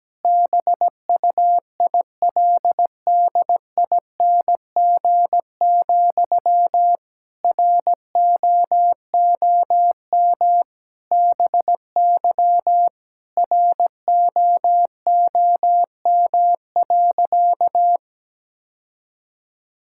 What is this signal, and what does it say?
Morse code, heard as a keyed tone, about 17 wpm, 700 Hz. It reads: BUILDING, ROOM BY ROOM.
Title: Sentences from Top 1000 Words - Review Part 5 - 17wpm